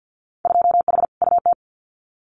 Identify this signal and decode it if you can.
Morse code, heard as a keyed tone, about 50 wpm, 700 Hz. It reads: 2H UT
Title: Winter Field Day Exchanges Encore 50wpm